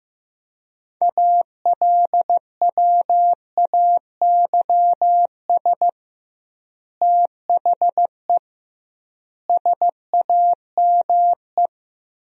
Morse code, heard as a keyed tone, about 15 wpm, 700 Hz. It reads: ALWAYS THE SAME
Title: 2x Word-Spacing - Sentences from Top 300 Words - 15wpm